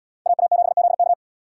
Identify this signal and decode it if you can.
Morse code, heard as a keyed tone, about 77 wpm, 700 Hz. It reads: AA7ZK